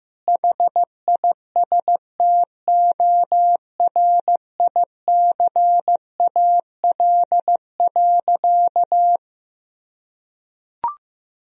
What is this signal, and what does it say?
Morse code, heard as a keyed tone, about 15 wpm, 700 Hz. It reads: HISTORICAL.
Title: Sentences from Top 1000 Words - Review Part 4 - 15wpm